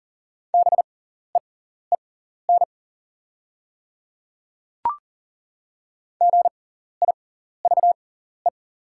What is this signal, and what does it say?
Morse code, heard as a keyed tone, about 40 wpm, spaced as for 14 wpm, 690 Hz. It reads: BEEN GIVE
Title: 6x Character Spacing - Top 500 Words - 40wpm